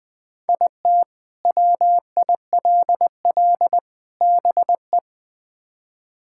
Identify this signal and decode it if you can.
Morse code, heard as a keyed tone, about 21 wpm, 700 Hz. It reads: IT WILL BE
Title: Build A Phrase - 3-Word Phrases - 20wpm